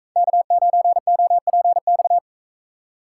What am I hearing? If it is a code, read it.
Morse code, heard as a keyed tone, about 44 wpm, 700 Hz. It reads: K9OPX